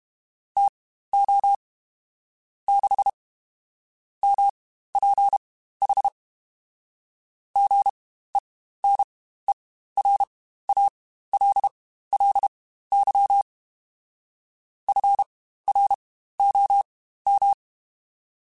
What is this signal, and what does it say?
Morse code, heard as a keyed tone, about 32 wpm, spaced as for 15 wpm, 800 Hz. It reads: TO 6 MPH GENERALLY FROM
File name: Spaced Characters / Spaced Characters-mp3s / Weather4 10 sp @30